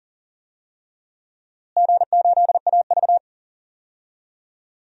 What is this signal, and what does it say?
Morse code, heard as a keyed tone, about 40 wpm, 700 Hz. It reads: G8AV
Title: DXCC-100 Call Signs - 4 Characters - Repeated 3x - 40wpm